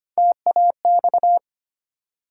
Morse code, heard as a keyed tone, about 25 wpm, 700 Hz. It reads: TAX
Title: Sentences from Top 1000 Words - Review Part 5 - New Word Drill - 25wpm